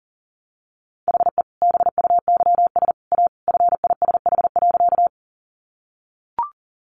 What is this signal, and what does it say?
Morse code, heard as a keyed tone, about 40 wpm, 700 Hz. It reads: HE BUYS A FISH.
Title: Additional Easy Phrases 40wpm